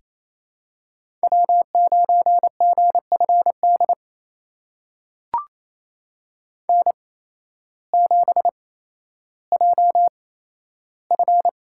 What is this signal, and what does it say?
Morse code, heard as a keyed tone, about 28 wpm, spaced as for 13 wpm, 700 Hz. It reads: W9GFD N7JF
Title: USA Call Signs - 8x Character Spacing - 28wpm